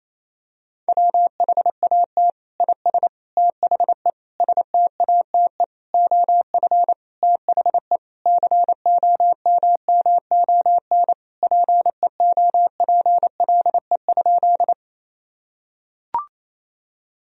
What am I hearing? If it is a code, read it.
Morse code, heard as a keyed tone, about 28 wpm, 700 Hz. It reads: WHAT IS THE STATE OF THE COMMON PEOPLE?
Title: Sentences from Top 1000 Words - Review Part 3 - 28wpm